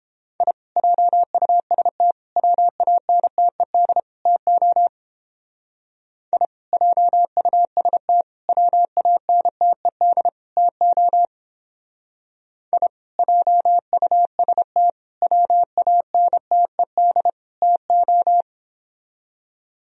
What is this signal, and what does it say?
Morse code, heard as a keyed tone, about 31 wpm, 700 Hz. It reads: I JUST WANTED TO I JUST WANTED TO I JUST WANTED TO